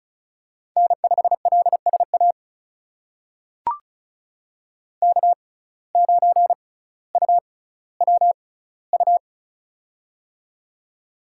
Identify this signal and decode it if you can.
Morse code, heard as a keyed tone, about 36 wpm, spaced as for 18 wpm, 700 Hz. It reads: N5LSA K9UWU
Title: USA Call Signs - 5 Characters - 6x Character Spacing - 35wpm